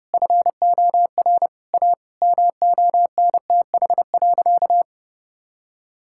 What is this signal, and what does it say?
Morse code, heard as a keyed tone, about 30 wpm, 700 Hz. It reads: FOR A MONTH.
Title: Sentences from Top 1000 Words - Review Part 5 - 30wpm